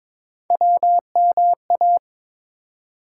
Morse code, heard as a keyed tone, about 22 wpm, 700 Hz. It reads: WMA
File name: practice / 22wpm / ARRL Sections v2 - 22wpm